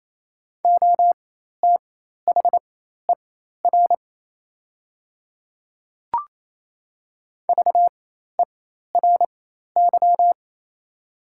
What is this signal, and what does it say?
Morse code, heard as a keyed tone, about 28 wpm, spaced as for 13 wpm, 700 Hz. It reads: OTHER VERY
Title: 4x Character Spacing - Top 200 Words - 28wpm